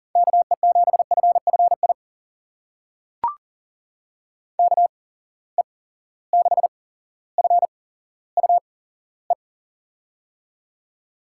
Morse code, heard as a keyed tone, about 42 wpm, spaced as for 17 wpm, 700 Hz. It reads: KE7FFI KE6FUE
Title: USA Call Signs - 6 Characters - 8x Character Spacing - 40wpm